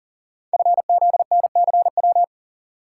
Morse code, heard as a keyed tone, about 38 wpm, 690 Hz. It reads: FZNCW
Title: Code Groups - 5 Letters - 40wpm